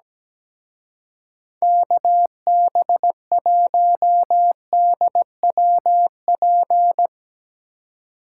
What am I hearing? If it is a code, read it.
Morse code, heard as a keyed tone, about 18 wpm, 700 Hz. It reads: KB1DWP
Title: Speed Racing - USA Call Signs 17wpm